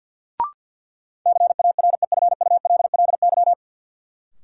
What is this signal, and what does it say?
Morse code, heard as a keyed tone, about 52 wpm, 700 Hz. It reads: CAREFULLY
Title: Sentences from Top 1300 Words - New Word Drill - 50wpm